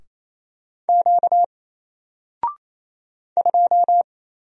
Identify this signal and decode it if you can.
Morse code, heard as a keyed tone, about 28 wpm, 700 Hz. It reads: Q 2
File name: Single-Letter-Number-28wpm